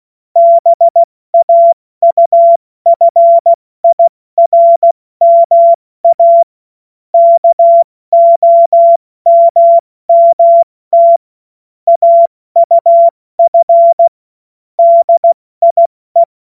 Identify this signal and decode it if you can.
Morse code, heard as a keyed tone, about 16 wpm, 680 Hz. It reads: BAUFIRMA KOMMT AUF DIE
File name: Morsekurs / Witze / Witz 27 Februar 2024 70 F80